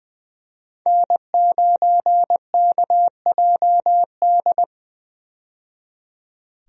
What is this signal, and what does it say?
Morse code, heard as a keyed tone, about 20 wpm, 700 Hz. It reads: N9KJD
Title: USA Call Signs - 5 Characters - 20wpm